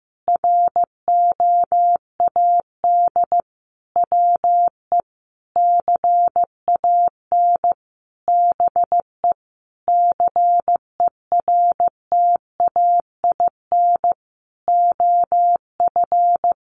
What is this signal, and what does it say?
Morse code, heard as a keyed tone, about 15 wpm, 700 Hz. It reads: ROAD WE CAN BE CERTAIN OF
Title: Sentences from Top 1100 Words 15wpm